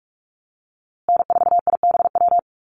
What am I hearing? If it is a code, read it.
Morse code, heard as a keyed tone, about 44 wpm, 700 Hz. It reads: N4IBW
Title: USA Call Signs Repeated 3x 45wpm